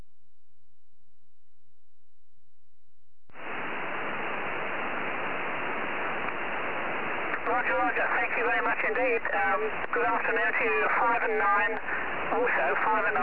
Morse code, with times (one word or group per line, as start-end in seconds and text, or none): none